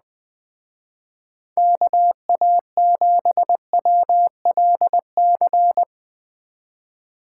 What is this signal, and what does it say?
Morse code, heard as a keyed tone, about 20 wpm, 700 Hz. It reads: KA7WLC
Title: Speed Racing - USA 2x3 Call Signs 17wpm